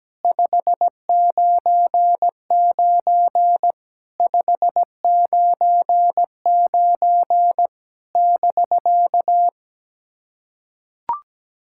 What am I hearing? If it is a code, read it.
Morse code, heard as a keyed tone, about 17 wpm, 690 Hz. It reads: 599 599 <BK>
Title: SOTA QSO Practice 17wpm